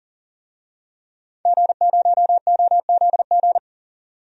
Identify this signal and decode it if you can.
Morse code, heard as a keyed tone, about 37 wpm, 700 Hz. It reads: G0OZG
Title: DXCC-100 Call Signs - 5 Characters - 40wpm